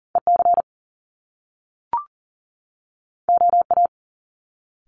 Morse code, heard as a keyed tone, about 40 wpm, 700 Hz. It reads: EC OA
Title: Top Two Letter Combinations in Words 40wpm